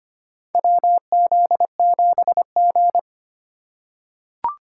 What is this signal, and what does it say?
Morse code, heard as a keyed tone, about 25 wpm, 700 Hz. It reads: WZ7G
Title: USA Call Signs - 4x Character Spacing - 25wpm